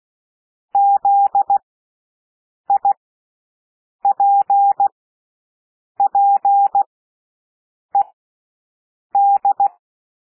Morse code, heard as a keyed tone, about 16 wpm, spaced as for 6 wpm, 800 Hz. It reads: ZIPPED
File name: cwmp3 / W1 SOUND WORDS 16 wpm